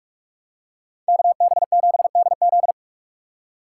Morse code, heard as a keyed tone, about 41 wpm, 700 Hz. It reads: KB7DZ